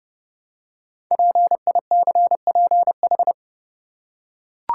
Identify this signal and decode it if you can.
Morse code, heard as a keyed tone, about 30 wpm, 690 Hz. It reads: PICPH
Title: Code Groups - 5 Letters - 30wpm